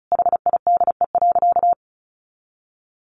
Morse code, heard as a keyed tone, about 34 wpm, 700 Hz. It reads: HIDE.
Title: Easy Phrases 35wpm